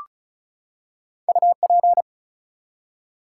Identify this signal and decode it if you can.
Morse code, heard as a keyed tone, about 36 wpm, 700 Hz. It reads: UP
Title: Two Letter Words - 35wpm